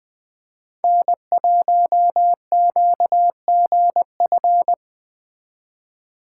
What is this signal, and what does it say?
Morse code, heard as a keyed tone, about 20 wpm, 700 Hz. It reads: N1QGF
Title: Speed Racing - USA 1x3 Call Signs 17wpm